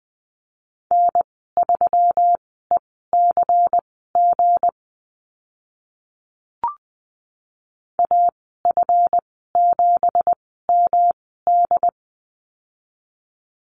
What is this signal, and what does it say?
Morse code, heard as a keyed tone, about 20 wpm, spaced as for 14 wpm, 700 Hz. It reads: N3ECG AF7MD
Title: 200 Lesson 32g - Letter k - Call Signs 20wpm